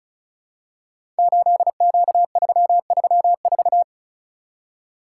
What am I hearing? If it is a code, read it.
Morse code, heard as a keyed tone, about 36 wpm, 700 Hz. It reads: 8Q334
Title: Code Groups - 5 Letter:Numbers - 35wpm